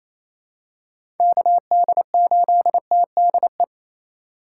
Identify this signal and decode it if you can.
Morse code, heard as a keyed tone, about 28 wpm, 700 Hz. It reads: KD8TDE
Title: USA Call Signs - 2x Character Spacing - 28wpm